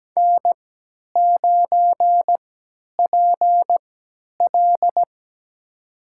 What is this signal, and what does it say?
Morse code, heard as a keyed tone, about 17 wpm, spaced as for 10 wpm, 700 Hz. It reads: N9PL
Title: USA Call Signs - 4 Characters - 3x Character Spacing - 17wpm